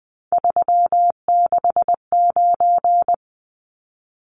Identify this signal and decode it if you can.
Morse code, heard as a keyed tone, about 20 wpm, 700 Hz. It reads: 369
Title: Numbers - 3-Digit 20wpm